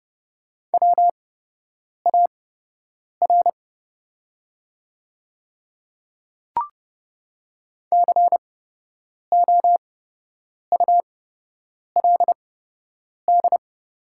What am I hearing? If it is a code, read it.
Morse code, heard as a keyed tone, about 30 wpm, spaced as for 8 wpm, 700 Hz. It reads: WAR COULD
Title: 8x Character Spacing - Top 500 Words - 30wpm